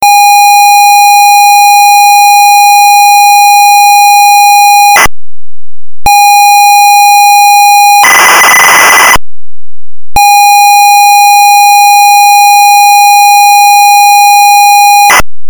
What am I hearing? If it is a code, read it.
Computer data on cassette.